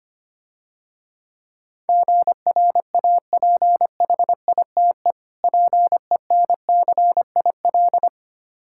Morse code, heard as a keyed tone, about 27 wpm, 700 Hz. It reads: GRAPHITE PENCIL
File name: practice / 25wpm / Difficult Words - Speed-Racing with Hints - 25wpm